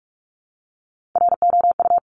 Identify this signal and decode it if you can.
Morse code, heard as a keyed tone, about 46 wpm, 700 Hz. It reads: ROU
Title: Top Three Letter Combinations in Words 45wpm